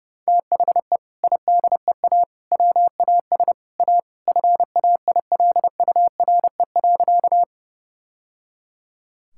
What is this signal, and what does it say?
Morse code, heard as a keyed tone, about 30 wpm, 700 Hz. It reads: THE IDEA WAS A FAILURE.
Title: Sentences from Top 1000 Words - Review Part 6 - 30wpm